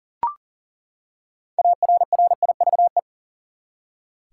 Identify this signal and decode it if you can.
Morse code, heard as a keyed tone, about 40 wpm, 700 Hz. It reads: ARRIVE
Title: Six Letter Words - 40wpm